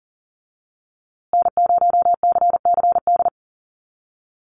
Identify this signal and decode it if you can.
Morse code, heard as a keyed tone, about 40 wpm, 700 Hz. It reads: N0CCD
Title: USA Call Signs Repeated 3x 40wpm